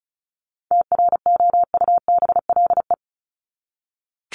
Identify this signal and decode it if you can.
Morse code, heard as a keyed tone, about 34 wpm, 700 Hz. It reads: TROUBLE
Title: Top 1001-1250 Words 35wpm